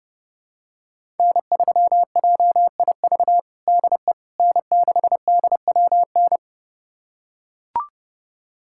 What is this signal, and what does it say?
Morse code, heard as a keyed tone, about 30 wpm, 700 Hz. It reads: N3JIV DE N6DWN
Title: USA Callsign DE Callsign - 5 Characters - 30wpm